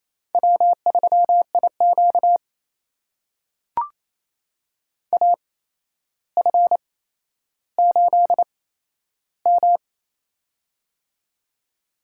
Morse code, heard as a keyed tone, about 28 wpm, spaced as for 12 wpm, 700 Hz. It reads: W3IQ AF8M
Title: USA Call Signs - 8x Character Spacing - 28wpm